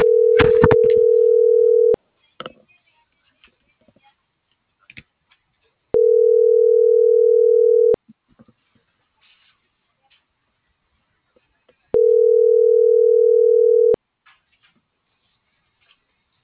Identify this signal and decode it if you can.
Ringback tone (North America), 440 + 480 Hz.